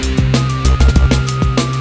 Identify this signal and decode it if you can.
There is no code.